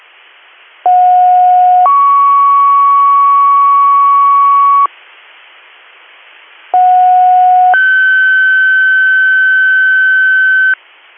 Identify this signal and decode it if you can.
Two-tone page, heard as a one-second tone, then a three-second tone, about 730 then 1120 Hz, about 730 then 1550 Hz. It is heard from 0.8 to 10.7 s.